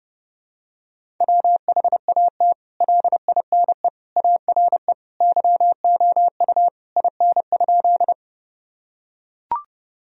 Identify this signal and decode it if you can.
Morse code, heard as a keyed tone, about 30 wpm, 700 Hz. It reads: WHAT LINE ARE YOU IN?
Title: Sentences from Top 1000 Words - Review Part 1 - 30wpm